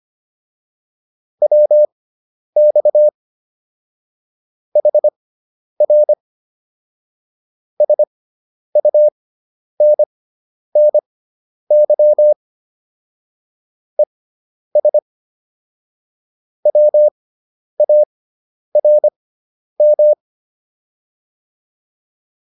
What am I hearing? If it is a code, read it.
Morse code, heard as a keyed tone, about 25 wpm, spaced as for 10 wpm, 600 Hz. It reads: WX HR SUNNY ES WARM
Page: intermediate-practice-files